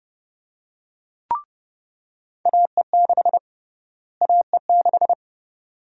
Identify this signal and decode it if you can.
Morse code, heard as a keyed tone, about 30 wpm, 700 Hz. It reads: AE6 AE6
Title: USA Call Sign Prefixes - Repeated 2x - 30wpm